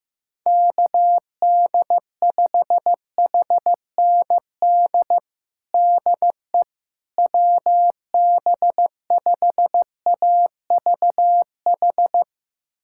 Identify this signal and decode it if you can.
Morse code, heard as a keyed tone, about 15 wpm, 700 Hz. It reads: KD5HND DE WB5AVH
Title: USA Callsign DE Callsign - 6 Characters - 15wpm